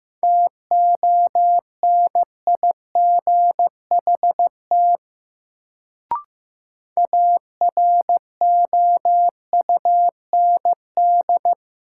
Morse code, heard as a keyed tone, about 16 wpm, 700 Hz. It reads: TONIGHT AROUND
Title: Sentences from Top 600 Words - New Word Drill - 15wpm